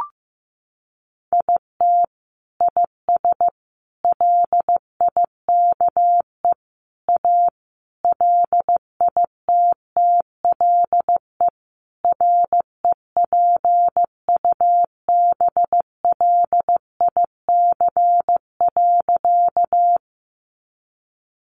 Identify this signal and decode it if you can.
Morse code, heard as a keyed tone, about 15 wpm, 700 Hz. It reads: IT IS LIKE A LITTLE REPUBLIC.
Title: Sentences from Top 1500 Words 15wpm